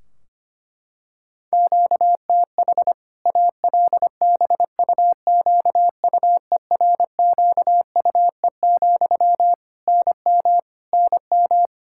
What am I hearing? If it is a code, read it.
Morse code, heard as a keyed tone, about 25 wpm, 700 Hz. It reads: QTH ALBUQUERQUE, NM NM
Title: K1USN SST QSO Practice 25wpm